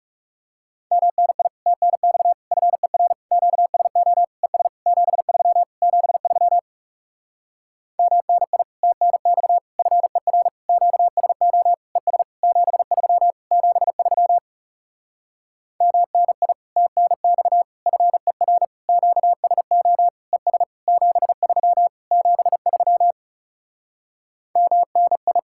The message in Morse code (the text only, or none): MNI TNX FER QSO ES 73 73 MNI TNX FER QSO ES 73 73 MNI TNX FER QSO ES 73 73 MNI